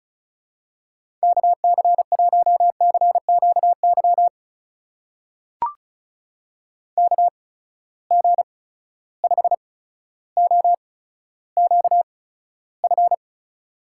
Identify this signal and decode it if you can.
Morse code, heard as a keyed tone, about 34 wpm, spaced as for 15 wpm, 700 Hz. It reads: KC1CQY KG5OQF
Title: USA Call Signs - 6 Characters - 8x Character Spacing - 35wpm